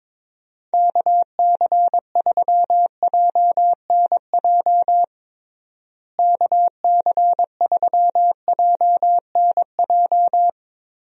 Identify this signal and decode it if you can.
Morse code, heard as a keyed tone, about 22 wpm, 700 Hz. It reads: KC3JNJ KC3JNJ
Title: USA Call Signs - 6 Characters - Repeated 2x - 22wpm